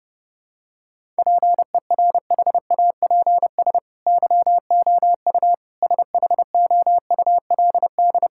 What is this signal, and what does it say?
Morse code, heard as a keyed tone, about 32 wpm, 700 Hz. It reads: PERHAPS YOU SHOULD
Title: Sentences from Top 1000 Words - Review Part 6 - 30wpm